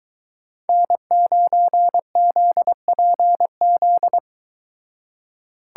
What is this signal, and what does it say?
Morse code, heard as a keyed tone, about 23 wpm, 700 Hz. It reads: N9ZPZ